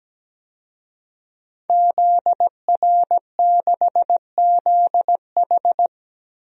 Morse code, heard as a keyed tone, about 17 wpm, 700 Hz. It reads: ZR6ZH
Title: DXCC-100 Call Signs - 5 Characters - Repeated 3x - 17wpm